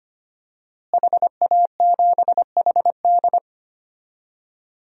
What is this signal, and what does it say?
Morse code, heard as a keyed tone, about 25 wpm, 700 Hz. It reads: HA7HD
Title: DXCC-100 Call Signs - 5 Characters - Repeated 3x - 25wpm